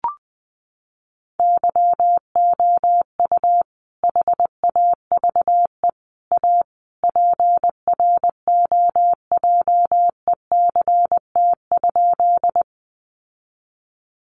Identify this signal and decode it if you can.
Morse code, heard as a keyed tone, about 20 wpm, 700 Hz. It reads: YOU HAVE A PROJECT?
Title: Sentences from Top 1400 Words 20wpm